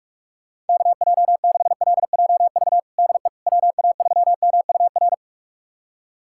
Morse code, heard as a keyed tone, about 41 wpm, 700 Hz. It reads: KJ6LJV DE WA3MUR
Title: USA Callsign DE Callsign - 6 Characters - 45wpm